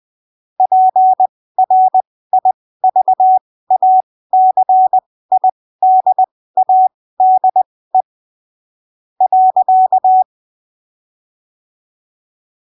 Morse code, heard as a keyed tone, about 20 wpm, spaced as for 13 wpm, 750 Hz. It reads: PRIVACIDADE.
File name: arquivos / texto / texto 30 20ppm